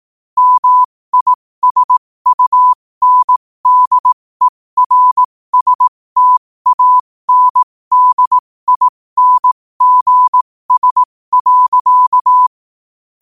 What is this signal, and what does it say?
Morse code, heard as a keyed tone, about 18 wpm, spaced as for 15 wpm, 1000 Hz. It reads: MISUNDERSTANDINGS.
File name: qotdfiles / 15wpm